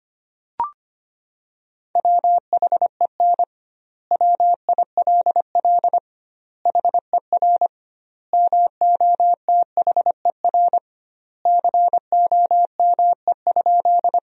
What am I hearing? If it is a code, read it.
Morse code, heard as a keyed tone, about 25 wpm, 700 Hz. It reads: WHEN WILL HER MOTHER COME?
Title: 2x Word-Spacing - Sentences from Top 200 Words - 25wpm